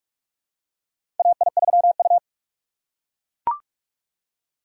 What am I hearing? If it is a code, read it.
Morse code, heard as a keyed tone, about 42 wpm, 700 Hz. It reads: AI3U
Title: USA Call Signs - 2x Character Spacing - 45wpm